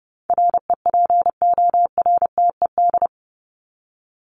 Morse code, heard as a keyed tone, about 30 wpm, 700 Hz. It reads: REPORTED
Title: ED Suffix - Top 100 Words with Base Words 30wpm